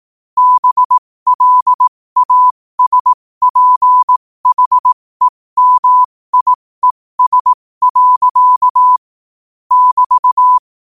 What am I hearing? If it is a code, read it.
Morse code, heard as a keyed tone, about 18 wpm, spaced as for 15 wpm, 1000 Hz. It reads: BLASPHEMIES. <BT>
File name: qotdfiles / 15wpm